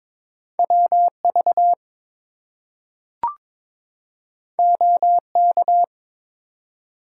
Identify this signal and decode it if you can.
Morse code, heard as a keyed tone, about 22 wpm, 700 Hz. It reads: WV OK
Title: US States and Canadian Provinces 22wpm